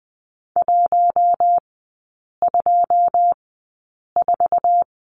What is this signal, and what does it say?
Morse code, heard as a keyed tone, about 20 wpm, spaced as for 8 wpm, 690 Hz. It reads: 124
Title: 093 Lesson 16e - Letter c - Three Characters 20wpm